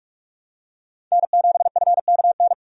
Morse code, heard as a keyed tone, about 46 wpm, 700 Hz. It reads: N7FKN
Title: USA Call Signs - 5 Characters - 45wpm